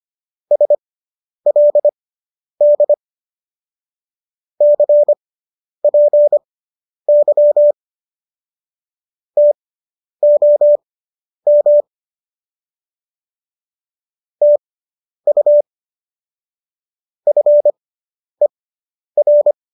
Morse code, heard as a keyed tone, about 25 wpm, spaced as for 10 wpm, 590 Hz. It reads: SLD CPY TOM TU FER